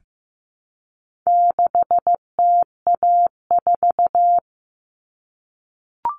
Morse code, heard as a keyed tone, about 15 wpm, 700 Hz. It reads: 6TA4
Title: All Cut Numbers - 4-Digit 15wpm